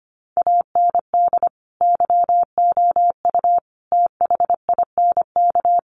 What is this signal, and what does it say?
Morse code, heard as a keyed tone, about 25 wpm, 700 Hz. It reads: AND YOU THINK
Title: Sentences from Top 1500 Words 25wpm